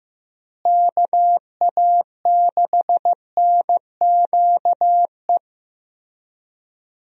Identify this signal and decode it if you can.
Morse code, heard as a keyed tone, about 15 wpm, 700 Hz. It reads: KA6NQE
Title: USA Call Signs - 6 Characters - Repeated 3x - 15wpm